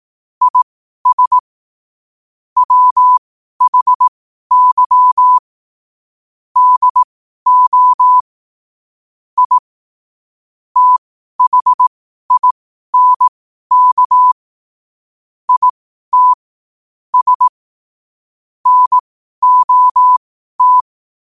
Morse code, heard as a keyed tone, about 18 wpm, spaced as for 12 wpm, 1000 Hz. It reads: IS WHY DO I THINK IT S NOT
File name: qotdfiles / 12wpm